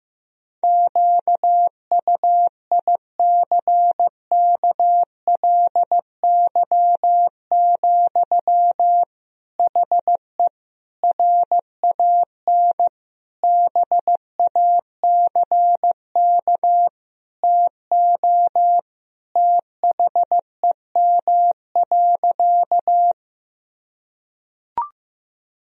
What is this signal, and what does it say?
Morse code, heard as a keyed tone, about 15 wpm, 700 Hz. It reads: QUICKLY, HE RAN BACK TO THEM.
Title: Sentences from Top 1000 Words - Review Part 3 - 15wpm